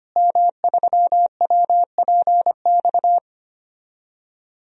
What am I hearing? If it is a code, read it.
Morse code, heard as a keyed tone, about 25 wpm, 700 Hz. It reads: M3WPX
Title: DXCC-100 Call Signs - 5 Characters - Repeated 3x - 25wpm